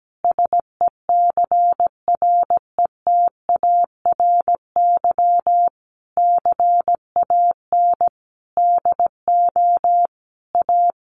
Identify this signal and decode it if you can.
Morse code, heard as a keyed tone, about 17 wpm, 700 Hz. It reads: SECRETARY CAN DO A
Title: Sentences from Top 1300 Words 17wpm